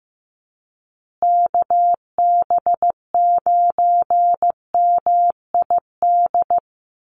Morse code, heard as a keyed tone, about 15 wpm, 700 Hz. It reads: KB9MID
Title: USA Call Signs Repeated 2x 15wpm